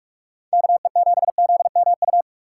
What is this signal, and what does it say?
Morse code, heard as a keyed tone, about 44 wpm, 700 Hz. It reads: KE7ZMU
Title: USA Call Signs - 6 Characters - 4x Character Spacing - 45wpm